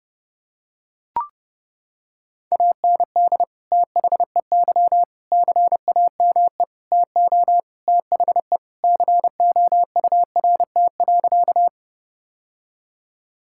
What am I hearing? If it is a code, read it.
Morse code, heard as a keyed tone, about 30 wpm, 700 Hz. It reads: AND THEY CAME TO THE COURT.
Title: Sentences from Top 1600 Words 30wpm